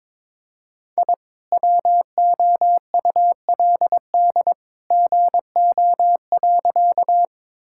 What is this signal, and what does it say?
Morse code, heard as a keyed tone, about 22 wpm, 700 Hz. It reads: I WOULD GO.
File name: practice / 22wpm / Sentences with 2-Word Phrases - 22wpm